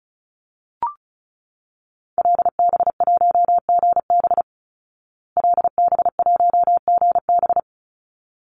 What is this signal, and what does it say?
Morse code, heard as a keyed tone, about 34 wpm, 700 Hz. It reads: LB1GB LB1GB
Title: Call Signs Repeated 3x - 35wpm